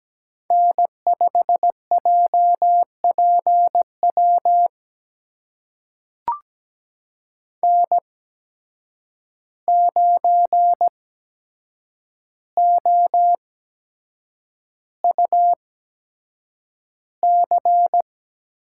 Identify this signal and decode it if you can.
Morse code, heard as a keyed tone, about 17 wpm, spaced as for 7 wpm, 700 Hz. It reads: N5JPW N9OUC
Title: USA Call Signs - 5 Characters - 8x Character Spacing - 17wpm